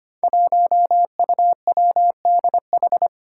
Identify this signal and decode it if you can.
Morse code, heard as a keyed tone, about 25 wpm, 700 Hz. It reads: 1UWDH
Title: Five Challenging Letter-Number Combinations 25wpm